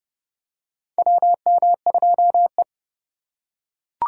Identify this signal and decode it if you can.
Morse code, heard as a keyed tone, about 31 wpm, 700 Hz. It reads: WM2E
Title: USA Call Signs - 4 Characters - 3x Character Spacing - 30wpm